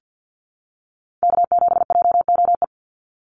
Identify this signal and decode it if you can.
Morse code, heard as a keyed tone, about 50 wpm, 700 Hz. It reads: K7JOE
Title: Call Signs - 50wpm